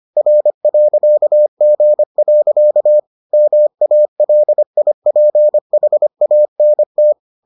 Morse code, heard as a keyed tone, about 25 wpm, 600 Hz. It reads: R.G. MALIPHANT